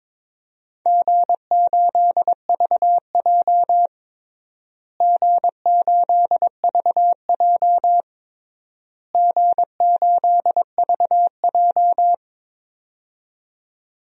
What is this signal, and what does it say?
Morse code, heard as a keyed tone, about 21 wpm, 700 Hz. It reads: G8VJ G8VJ G8VJ
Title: DXCC-100 Call Signs - 4 Characters - Repeated 3x - 22wpm